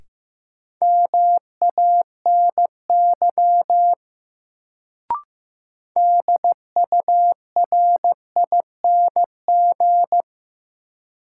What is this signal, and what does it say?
Morse code, heard as a keyed tone, about 15 wpm, 700 Hz. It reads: MANY DURING
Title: Sets of 2 Words - New Word Drill - 15wpm